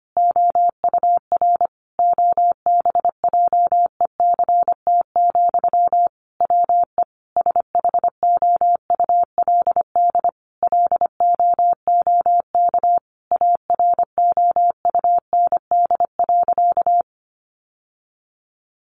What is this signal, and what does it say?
Morse code, heard as a keyed tone, about 25 wpm, 700 Hz. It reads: OUR OBJECT, WE SHOULD LOOK AROUND.
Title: Sentences from Top 800 Words 25wpm